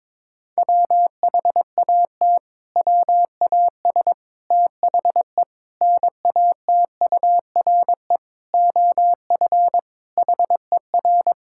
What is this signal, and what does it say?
Morse code, heard as a keyed tone, about 22 wpm, 700 Hz. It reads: WHAT WAS THE NATURE OF HER
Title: Sentences from Top 1000 Words - Review Part 6 - 22wpm